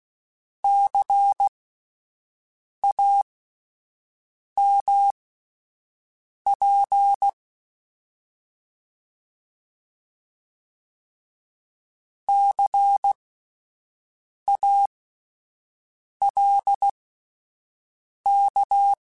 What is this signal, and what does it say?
Morse code, heard as a keyed tone, about 16 wpm, spaced as for 5 wpm, 800 Hz. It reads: CAMP CALK
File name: Spaced Alphas / Spaced Alphas mp3s / Cs sp @10